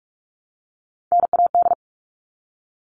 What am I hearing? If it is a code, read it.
Morse code, heard as a keyed tone, about 44 wpm, 700 Hz. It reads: NAD
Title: All Cut Numbers - 3-Digit 45wpm